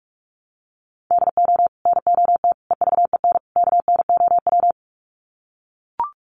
Morse code, heard as a keyed tone, about 44 wpm, 700 Hz. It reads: DO NOT EVEN KNOW
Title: Sets of 4 Words 45wpm